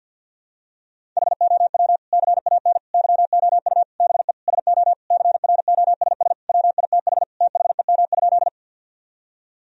Morse code, heard as a keyed tone, about 51 wpm, 700 Hz. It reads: HOW CAN YOU BE SO CROSS WITH THEM?
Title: Sentences from Top 1000 Words - Review Part 1 - 50wpm